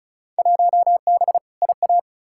Morse code, heard as a keyed tone, about 36 wpm, 700 Hz. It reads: 1B IA
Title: Field Day Exchanges Encore - 35wpm